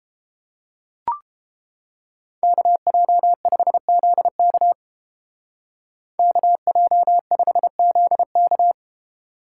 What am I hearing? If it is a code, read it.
Morse code, heard as a keyed tone, about 31 wpm, 700 Hz. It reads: KJ5ZK KJ5ZK